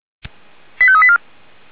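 Five-tone selective call (ZVEI), heard as quick runs of five tones, tones of about 70 ms. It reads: 85384
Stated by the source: ZVEI five-tone selective call received by SDR.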